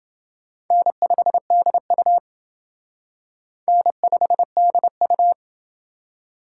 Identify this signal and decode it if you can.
Morse code, heard as a keyed tone, about 28 wpm, 700 Hz. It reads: N5DU N5DU